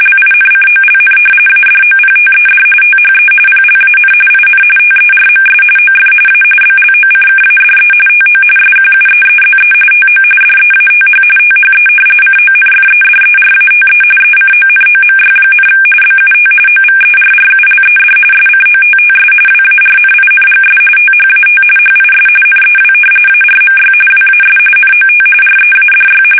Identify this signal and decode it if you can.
RTTY, about 45.45 baud. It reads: LINCOLN ADOPTED THE PLAN IN TERMS OF A BLOCKADE TO SQUEEZE TO DEATH THE CONFEDERATE ECONOMY, BUT OVERRULED SCOTTS WARNINGS THAT HIS NEW ARMY WAS NOT READY
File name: fsk850 / usnavy3